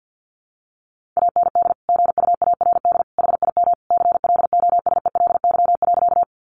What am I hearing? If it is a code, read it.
Morse code, heard as a keyed tone, about 49 wpm, 700 Hz. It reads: AND GUARD HIM CLOSELY.
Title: Sentences from Top 1400 Words 50wpm